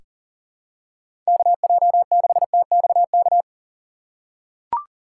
Morse code, heard as a keyed tone, about 40 wpm, 700 Hz. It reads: KJ6TXK
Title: USA Call Signs - 6 Characters - 3x Character Spacing - 40wpm